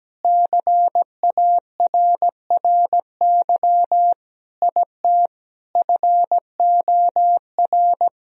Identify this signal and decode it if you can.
Morse code, heard as a keyed tone, about 17 wpm, 700 Hz. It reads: CARRY IT FOR
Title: Sentences from Top 1000 Words - Review Part 1 - 17wpm